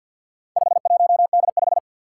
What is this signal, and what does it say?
Morse code, heard as a keyed tone, about 48 wpm, 700 Hz. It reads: 51D5
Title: Four Challenging Letter-Number Combinations 50wpm